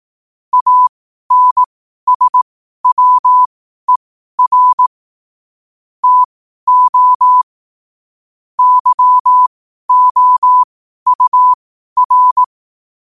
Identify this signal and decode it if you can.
Morse code, heard as a keyed tone, about 18 wpm, spaced as for 12 wpm, 1000 Hz. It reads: ANSWER TO YOUR
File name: qotdfiles / 12wpm